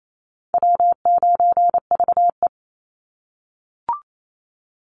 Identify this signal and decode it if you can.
Morse code, heard as a keyed tone, about 28 wpm, 700 Hz. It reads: W9VE
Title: Call Signs - 28wpm